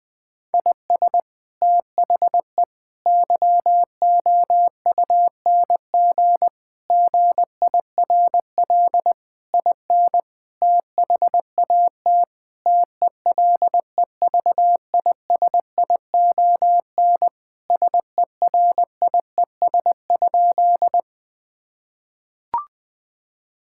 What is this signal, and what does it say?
Morse code, heard as a keyed tone, about 20 wpm, 700 Hz. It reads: IS THE YOUNG GIRL IN THAT TELEVISION SERIES?
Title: Sentences from Top 1000 Words - Review Part 5 - 20wpm